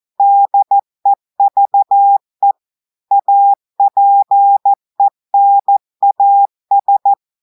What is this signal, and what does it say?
Morse code, heard as a keyed tone, about 14 wpm, 800 Hz. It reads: DEVE APENAS
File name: audios / portugais / portugais texte en clair 2 vitesse 14 mots minutes